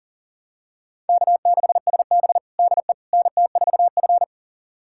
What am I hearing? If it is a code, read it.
Morse code, heard as a keyed tone, about 40 wpm, 700 Hz. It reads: K6SB DE NT4F
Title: USA Callsign DE Callsign - 4 Characters - 40wpm